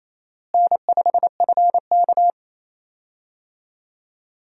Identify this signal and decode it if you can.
Morse code, heard as a keyed tone, about 27 wpm, 700 Hz. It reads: N5FK
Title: USA Call Signs - 4 Characters - Repeated 2x - 28wpm